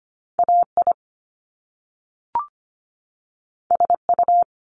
Morse code, heard as a keyed tone, about 25 wpm, 700 Hz. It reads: AI SU
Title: Top Two Letter Combinations in Words 25wpm